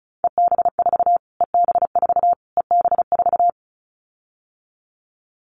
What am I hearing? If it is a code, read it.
Morse code, heard as a keyed tone, about 34 wpm, 700 Hz. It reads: EB4 EB4 EB4
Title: Callsign Prefixes for Top 25 DXCC - Repeated 3x - 35wpm